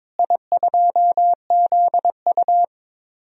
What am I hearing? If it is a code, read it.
Morse code, heard as a keyed tone, about 22 wpm, 700 Hz. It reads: I2ZU
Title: DXCC-100 Call Signs - 4 Characters - Repeated 3x - 22wpm